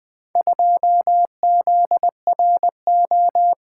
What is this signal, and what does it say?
Morse code, heard as a keyed tone, about 20 wpm, 700 Hz. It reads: 2ZRO
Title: Four Letter-Number Combinations - 20wpm